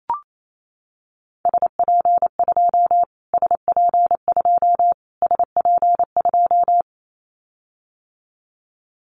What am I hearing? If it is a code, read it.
Morse code, heard as a keyed tone, about 28 wpm, 700 Hz. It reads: SP2 SP2 SP2
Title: Callsign Prefixes for Top 25 DXCC - Repeated 3x - 28wpm